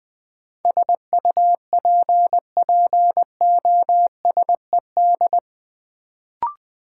Morse code, heard as a keyed tone, about 20 wpm, 700 Hz. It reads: SUPPOSED
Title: Sets of 3 Words - New Word Drill - 20wpm